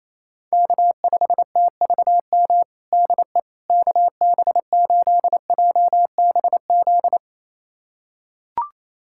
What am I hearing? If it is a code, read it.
Morse code, heard as a keyed tone, about 28 wpm, 700 Hz. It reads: K5TVM DE KB8JBZ
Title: USA Callsign DE Callsign 28wpm